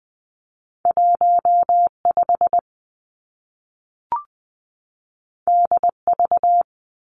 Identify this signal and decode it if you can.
Morse code, heard as a keyed tone, about 20 wpm, 700 Hz. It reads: 1 5 D V
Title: Two Challenging Letter-Number Combinations 20wpm